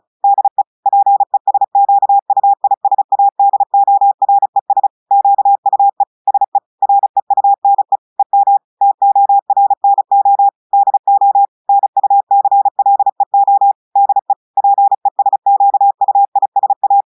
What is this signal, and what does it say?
Morse code, heard as a keyed tone, about 36 wpm, 800 Hz. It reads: DE PESQUISADORES QUE SE REUNE EM TORNO DO NUCLEO DE PESQUISA
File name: audios / portugais / portugais texte en clair 1 vitesse 35 mots minutes